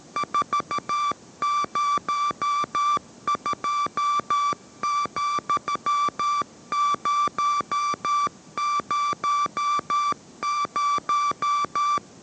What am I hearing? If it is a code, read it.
Morse code, heard as a keyed tone, about 13 wpm, 1170 Hz. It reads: 402,000